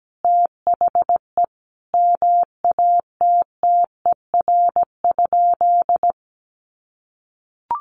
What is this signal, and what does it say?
Morse code, heard as a keyed tone, about 17 wpm, 700 Hz. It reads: THE MATTER?
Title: Sentences from Top 700 Words 17wpm